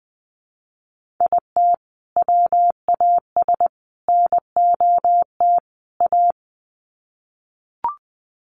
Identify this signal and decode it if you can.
Morse code, heard as a keyed tone, about 20 wpm, 700 Hz. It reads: IT WAS NOT A
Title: Sets of 4 Words 20wpm